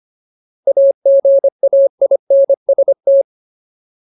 Morse code, heard as a keyed tone, about 25 wpm, 550 Hz. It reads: AGAINST
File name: Ultimate Word List / 25WPM Ultimate